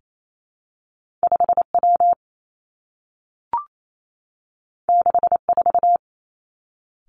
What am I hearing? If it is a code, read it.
Morse code, heard as a keyed tone, about 28 wpm, 700 Hz. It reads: 5 W 6 4
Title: Two Challenging Letter-Number Combinations 28wpm